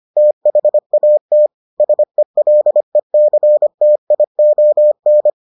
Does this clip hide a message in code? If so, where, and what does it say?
Yes: Morse code from 0.2 to 5.3 s, heard as a keyed tone, about 25 wpm, 600 Hz. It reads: THAT SELECTION